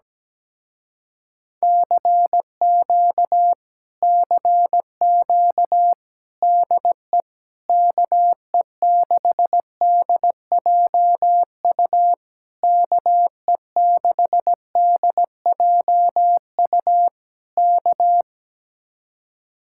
Morse code, heard as a keyed tone, about 17 wpm, 700 Hz. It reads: CQ CQ DE KE6DJU KE6DJU K
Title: Speed-Racing Common QSO Phrases - 17wpm